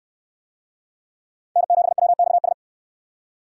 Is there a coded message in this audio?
Yes: Morse code from 1.6 to 2.5 s, heard as a keyed tone, about 66 wpm, 700 Hz. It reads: N7WXS